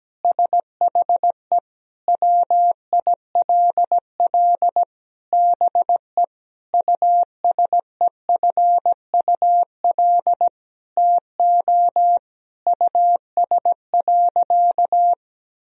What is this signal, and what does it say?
Morse code, heard as a keyed tone, about 17 wpm, 700 Hz. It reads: SHE WILL BE USEFUL TO US.
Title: Sentences from Top 1000 Words - Review Part 4 - 17wpm